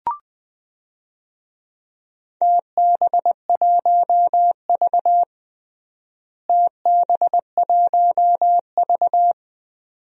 Morse code, heard as a keyed tone, about 20 wpm, 700 Hz. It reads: TB1V TB1V
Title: DXCC-100 Call Signs - 4 Characters - Repeated 2x - 20wpm